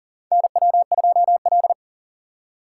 Morse code, heard as a keyed tone, about 40 wpm, 700 Hz. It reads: NW2L
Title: USA Call Signs - 4 Characters - 6x Character Spacing - 40wpm